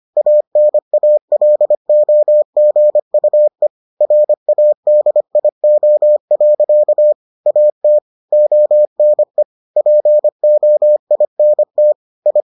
Morse code, heard as a keyed tone, about 25 wpm, 600 Hz. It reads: ANALOGUE RADIO. AT ONE POINT I